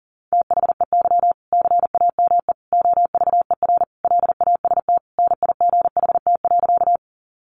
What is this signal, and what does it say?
Morse code, heard as a keyed tone, about 40 wpm, 700 Hz. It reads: THEY CAME OVER LAST NIGHT.
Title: Sentences from Top 300 Words 40wpm